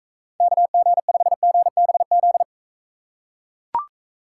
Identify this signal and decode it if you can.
Morse code, heard as a keyed tone, about 39 wpm, 700 Hz. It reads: KG5GBZ